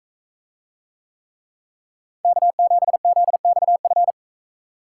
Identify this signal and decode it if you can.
Morse code, heard as a keyed tone, about 44 wpm, 700 Hz. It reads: K7ZXF